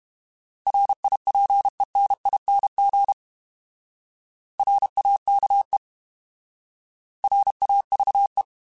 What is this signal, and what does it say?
Morse code, heard as a keyed tone, about 32 wpm, 800 Hz. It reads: RIPENING RAKE RAVE
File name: Alphabetical / Alphabetical mp3s / Rs @30